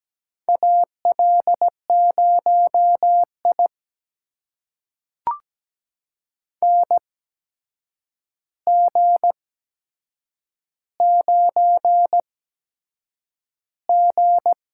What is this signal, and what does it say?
Morse code, heard as a keyed tone, about 17 wpm, spaced as for 7 wpm, 700 Hz. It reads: AL0I NG9G
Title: USA Call Signs - 4 Characters - 8x Character Spacing - 17wpm